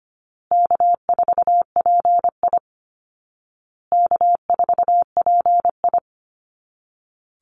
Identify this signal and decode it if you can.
Morse code, heard as a keyed tone, about 25 wpm, 700 Hz. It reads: K4PI K4PI
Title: Easy Call Signs Repeated 3x - 25wpm